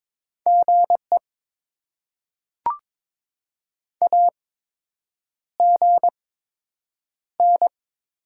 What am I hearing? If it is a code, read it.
Morse code, heard as a keyed tone, about 22 wpm, spaced as for 8 wpm, 700 Hz. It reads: GE AGN
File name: Top 20 QSO Elements - 8x Character Spacing - 22wpm